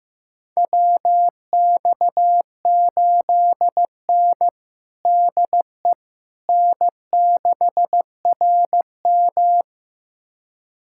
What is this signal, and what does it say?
Morse code, heard as a keyed tone, about 15 wpm, 700 Hz. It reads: WX8N DE N6RM
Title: USA Callsign DE Callsign - 4 Characters - 15wpm